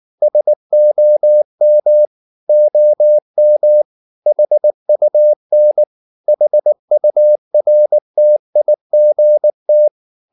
Morse code, heard as a keyed tone, about 19 wpm, 600 Hz. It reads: SOM OM HUN HURTIGT
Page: Ferie 19 wpm